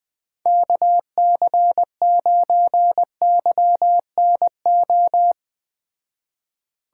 Morse code, heard as a keyed tone, about 20 wpm, 700 Hz. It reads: KC9YNO